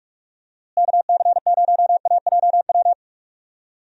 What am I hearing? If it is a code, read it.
Morse code, heard as a keyed tone, about 43 wpm, 700 Hz. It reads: KC0AJW
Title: USA Call Signs - 6 Characters - Repeated 3x - 45wpm